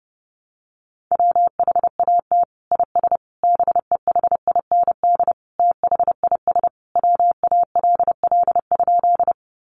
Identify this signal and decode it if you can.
Morse code, heard as a keyed tone, about 30 wpm, 700 Hz. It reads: WHAT IS BEHIND THIS WALL?
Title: Sentences from Top 1100 Words 30wpm